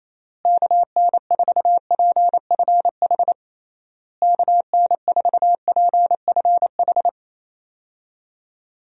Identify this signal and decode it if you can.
Morse code, heard as a keyed tone, about 28 wpm, 700 Hz. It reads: KN4PFH KN4PFH
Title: USA Call Signs - 6 Characters - Repeated 3x - 28wpm